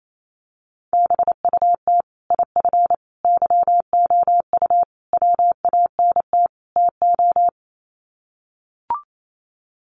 Morse code, heard as a keyed tone, about 28 wpm, 700 Hz. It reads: BUT IF YOU WANT TO
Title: Sets of 5 Words Encore2 28wpm